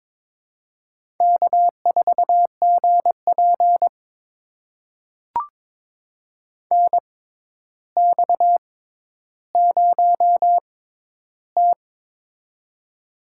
Morse code, heard as a keyed tone, about 22 wpm, spaced as for 11 wpm, 700 Hz. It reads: K4GP NX0T
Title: USA Call Signs - 4 Characters - 6x Character Spacing - 22wpm